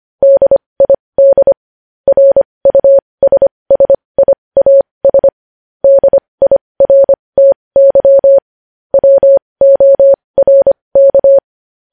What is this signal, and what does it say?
Morse code, heard as a keyed tone, about 25 wpm, spaced as for 20 wpm, 550 Hz. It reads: DID RUSSIAS DIRTY WORK
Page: news-headlines-in-morse-code-at-20-wpm